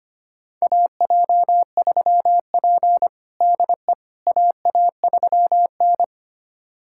Morse code, heard as a keyed tone, about 25 wpm, 700 Hz. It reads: AJ3P DE AA3N
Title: USA Callsign DE Callsign - 4 Characters - 25wpm